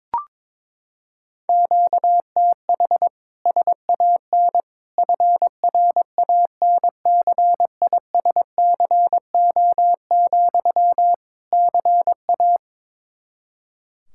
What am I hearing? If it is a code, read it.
Morse code, heard as a keyed tone, about 22 wpm, 700 Hz. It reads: QTH SAN FRANCISCO, CA
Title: Practice QSO - Exchange 1 - Beginning - 22wpm